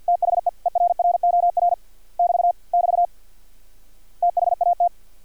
Morse code, heard as a keyed tone, about 50 wpm, 700 Hz. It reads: THE ERROR <BT> <BT> THAT
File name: archived / CRSnet